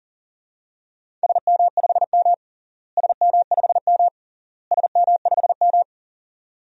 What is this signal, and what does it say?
Morse code, heard as a keyed tone, about 40 wpm, 700 Hz. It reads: SM5M SM5M SM5M
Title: DXCC-100 Call Signs - 4 Characters - Repeated 3x - 40wpm